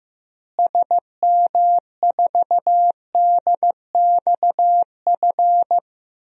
Morse code, heard as a keyed tone, about 15 wpm, 700 Hz. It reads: SM4DXF
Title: DXCC-100 Call Signs - 6 Characters - Repeated 2x - 15wpm